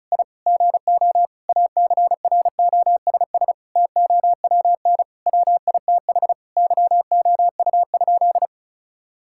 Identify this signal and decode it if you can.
Morse code, heard as a keyed tone, about 36 wpm, 700 Hz. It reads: I GO ACROSS TOWN WITH YOU?
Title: Sentences from Top 1000 Words - Review Part 5 - 35wpm